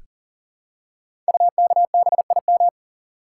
Morse code, heard as a keyed tone, about 38 wpm, 700 Hz. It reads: UKBIM
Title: Code Groups - 5 Letters - 40wpm